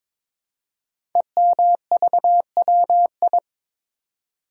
Morse code, heard as a keyed tone, about 22 wpm, 690 Hz. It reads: EMVWI
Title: Code Groups - 5 Letters - 22wpm